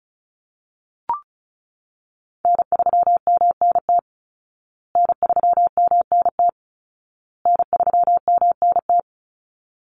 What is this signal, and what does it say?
Morse code, heard as a keyed tone, about 34 wpm, 700 Hz. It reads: N3MNT N3MNT N3MNT
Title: Call Signs Repeated 2x - 35wpm